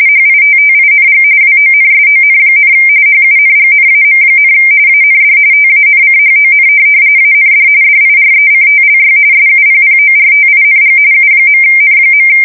RTTY, about 45.45 baud. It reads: HITTING WONSON, HUNGNAM, AND TANCHON AND DIRECTLY SUPPORTING U.N.
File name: fsk170 / USS-New-Jersey